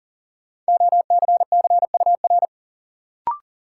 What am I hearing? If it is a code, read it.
Morse code, heard as a keyed tone, about 40 wpm, 700 Hz. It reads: OCCUR
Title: Sentences from Top 900 Words - New Word Drill - 40wpm